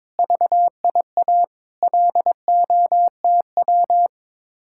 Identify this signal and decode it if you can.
Morse code, heard as a keyed tone, about 22 wpm, 700 Hz. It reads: VIA LOTW
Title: Speed-Racing Common QSO Phrases - 22wpm